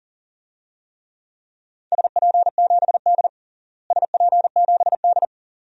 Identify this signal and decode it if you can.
Morse code, heard as a keyed tone, about 40 wpm, 700 Hz. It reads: SP7D SP7D
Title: DXCC-100 Call Signs - 4 Characters - Repeated 2x - 40wpm